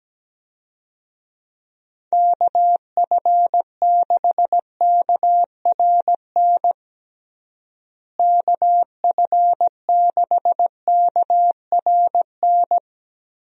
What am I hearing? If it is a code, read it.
Morse code, heard as a keyed tone, about 17 wpm, 700 Hz. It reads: KF6KRN KF6KRN
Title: USA Call Signs - 6 Characters - Repeated 2x - 17wpm